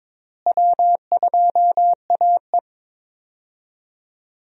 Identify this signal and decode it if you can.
Morse code, heard as a keyed tone, about 22 wpm, 700 Hz. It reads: W2AE
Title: USA Call Signs - 4 Characters - Repeated 2x - 22wpm